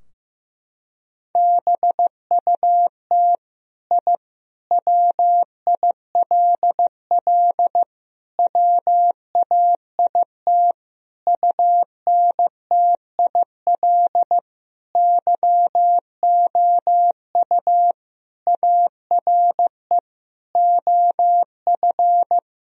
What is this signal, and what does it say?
Morse code, heard as a keyed tone, about 16 wpm, 700 Hz. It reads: BUT I WILL WAIT UNTIL YOU ARE OF
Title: Sentences from Top 1000 Words - Review Part 3 - 15wpm